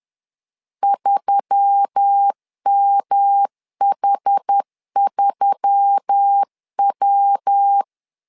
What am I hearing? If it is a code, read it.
Morse code, heard as a keyed tone, about 10 wpm, 800 Hz. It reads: 3MH3W